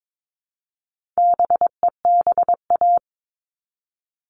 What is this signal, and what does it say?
Morse code, heard as a keyed tone, about 22 wpm, 700 Hz. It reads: BEBA
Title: All Cut Numbers - 4-Digit 22wpm